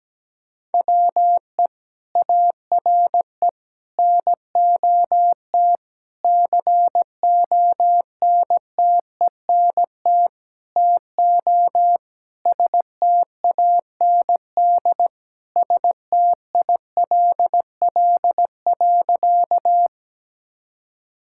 Morse code, heard as a keyed tone, about 17 wpm, 700 Hz. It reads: WE ARE NOT CONTENT TO STAND STILL.
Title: Sentences from Top 1000 Words - Review Part 6 - 17wpm